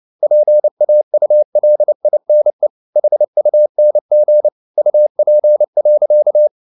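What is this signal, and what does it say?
Morse code, heard as a keyed tone, about 29 wpm, 600 Hz. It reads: PAULINE HUNG UP.
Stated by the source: Morse code at 29 wpm, sending Ken Follett's novel Never in English, from CW med Gnister.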